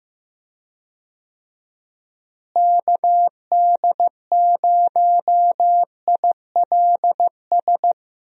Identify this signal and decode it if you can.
Morse code, heard as a keyed tone, about 16 wpm, 700 Hz. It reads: KD0ILS
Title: USA Call Signs - 6 Characters - Repeated 3x - 15wpm